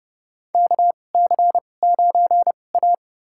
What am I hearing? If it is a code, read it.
Morse code, heard as a keyed tone, about 30 wpm, spaced as for 22 wpm, 700 Hz. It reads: KC9A
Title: USA Call Signs - 4 Characters - 2x Character Spacing - 30wpm